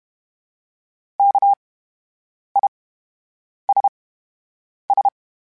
Morse code, heard as a keyed tone, about 32 wpm, spaced as for 8 wpm, 800 Hz. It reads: KISS
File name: Special INGs / Special INGs-mp3s / VerySlowToFaster Ing @30